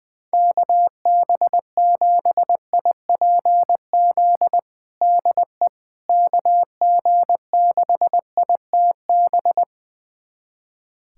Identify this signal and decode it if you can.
Morse code, heard as a keyed tone, about 20 wpm, 700 Hz. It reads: KB7IPZ DE KG6ITB
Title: USA Callsign DE Callsign 20wpm